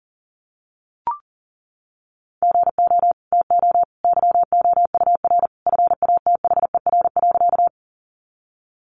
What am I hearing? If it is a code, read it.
Morse code, heard as a keyed tone, about 40 wpm, 700 Hz. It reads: GO TO YOUR FATHER.
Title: Sentences from Top 300 Words 40wpm